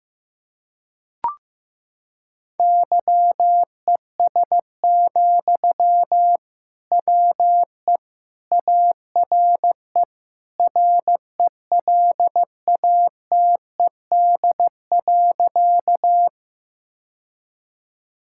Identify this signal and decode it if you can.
Morse code, heard as a keyed tone, about 15 wpm, 690 Hz. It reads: YES, WE ARE RELATED.
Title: Sentences from Top 1000 Words - Review Part 5 - 15wpm